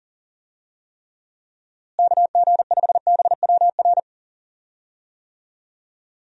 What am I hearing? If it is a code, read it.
Morse code, heard as a keyed tone, about 38 wpm, 700 Hz. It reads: KG5BWR
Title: USA Call Signs - 6 Characters - Repeated 3x - 40wpm